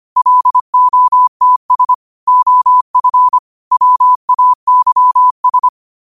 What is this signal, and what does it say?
Morse code, heard as a keyed tone, about 25 wpm, 1000 Hz. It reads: LOTS OF WAYS
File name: qotdfiles / 25wpm